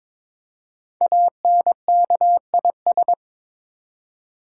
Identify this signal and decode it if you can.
Morse code, heard as a keyed tone, about 22 wpm, 690 Hz. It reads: ANKIS
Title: Code Groups - 5 Letters - 22wpm